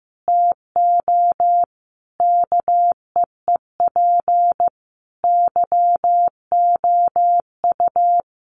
Morse code, heard as a keyed tone, about 15 wpm, 700 Hz. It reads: TO KEEP YOU
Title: A Sound Of Thunder 15wpm